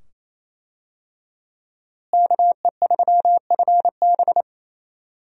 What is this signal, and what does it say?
Morse code, heard as a keyed tone, about 28 wpm, 700 Hz. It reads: KE3FB
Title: USA Call Signs - 5 Characters - Repeated 2x - 28wpm